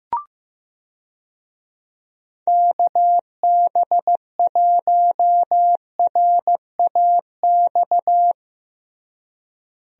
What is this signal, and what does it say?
Morse code, heard as a keyed tone, about 15 wpm, 700 Hz. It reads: KB1RAX
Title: USA Call Signs - 6 Characters - Repeated 2x - 15wpm